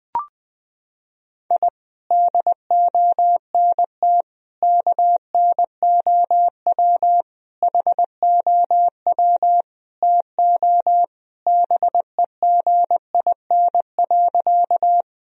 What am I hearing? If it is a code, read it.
Morse code, heard as a keyed tone, about 20 wpm, 700 Hz. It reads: I DONT KNOW HOW TO BEGIN.
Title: Sentences from Top 1000 Words - Review Part 1 - 20wpm